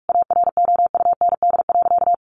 Morse code, heard as a keyed tone, about 44 wpm, 700 Hz. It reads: AROUND.
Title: Sentences from Top 1300 Words 45wpm